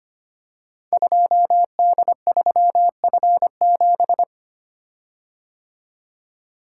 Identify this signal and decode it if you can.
Morse code, heard as a keyed tone, about 25 wpm, 700 Hz. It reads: 2D3F7
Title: Code Groups - 5 Letter:Numbers - 25wpm